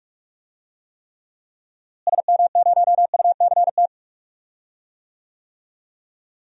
Morse code, heard as a keyed tone, about 46 wpm, 700 Hz. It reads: SM0UCT
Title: DXCC-100 Call Signs - 6 Characters - Repeated 3x - 45wpm